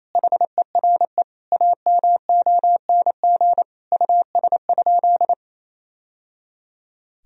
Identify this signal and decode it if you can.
Morse code, heard as a keyed tone, about 28 wpm, 700 Hz. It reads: HERE AMONG US?
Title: Sentences from Top 1000 Words - Review Part 3 - 28wpm